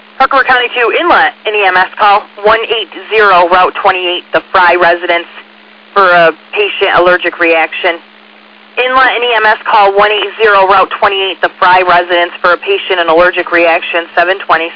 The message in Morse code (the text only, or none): none